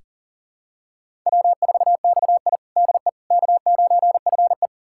Morse code, heard as a keyed tone, about 40 wpm, 700 Hz. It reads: W4XI DE K9FE
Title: USA Callsign DE Callsign - 4 Characters - 40wpm